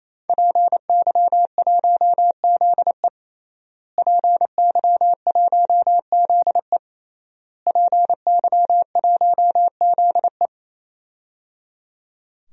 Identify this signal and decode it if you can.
Morse code, heard as a keyed tone, about 28 wpm, 700 Hz. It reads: PY1ZE PY1ZE PY1ZE
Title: DXCC-100 Call Signs - 5 Characters - Repeated 3x - 28wpm